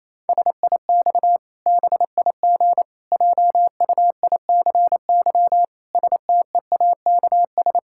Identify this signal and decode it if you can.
Morse code, heard as a keyed tone, about 27 wpm, 700 Hz. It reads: SIX BIG JUICY STEAKS
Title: Pangram Challenge v2 - 28wpm